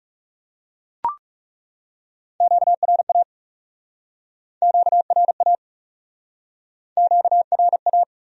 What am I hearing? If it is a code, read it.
Morse code, heard as a keyed tone, about 40 wpm, 700 Hz. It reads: QRA QRA QRA